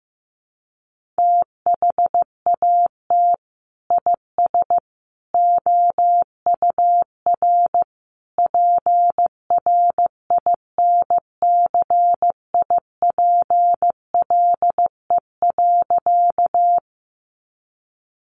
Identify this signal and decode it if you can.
Morse code, heard as a keyed tone, about 15 wpm, 700 Hz. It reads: THAT IS OUR PRINCIPLE.
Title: Sentences from Top 1400 Words 15wpm